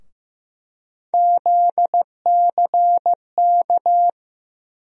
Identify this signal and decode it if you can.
Morse code, heard as a keyed tone, about 16 wpm, 700 Hz. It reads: ZCK
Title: USA Call Sign Postfixes - 3 Letters 15wpm